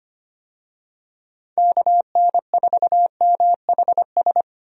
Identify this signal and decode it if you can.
Morse code, heard as a keyed tone, about 25 wpm, 700 Hz. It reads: KN4MHS
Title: USA Call Signs - 6 Characters - 25wpm